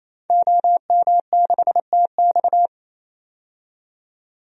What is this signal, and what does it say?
Morse code, heard as a keyed tone, about 28 wpm, 700 Hz. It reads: OM6TX
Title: DXCC-100 Call Signs - 5 Characters - 28wpm